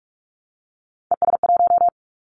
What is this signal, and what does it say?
Morse code, heard as a keyed tone, about 44 wpm, 700 Hz. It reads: ES1
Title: Callsign Prefixes for Top 50 DXCC - 45wpm